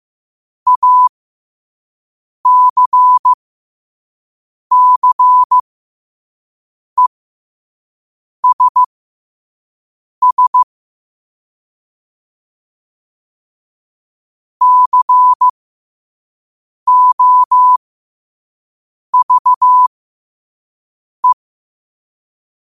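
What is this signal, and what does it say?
Morse code, heard as a keyed tone, about 15 wpm, spaced as for 5 wpm, 1000 Hz. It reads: ACCESS COVE
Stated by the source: Morse code quote of the day sent at 5 wpm.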